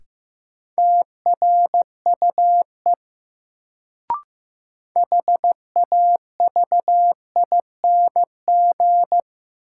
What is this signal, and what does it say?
Morse code, heard as a keyed tone, about 15 wpm, 700 Hz. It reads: TRUE HAVING
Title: Sets of 5 Words Encore - New Word Drill - 15wpm